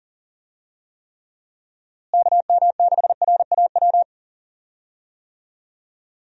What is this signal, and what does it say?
Morse code, heard as a keyed tone, about 40 wpm, 700 Hz. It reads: KM6RAW
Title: USA Call Signs - 6 Characters - Repeated 3x - 40wpm